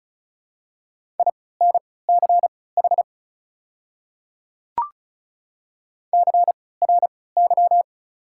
Morse code, heard as a keyed tone, about 34 wpm, spaced as for 20 wpm, 690 Hz. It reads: INCH CRY
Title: 3x Character Spacing - Top 500 Words - 35wpm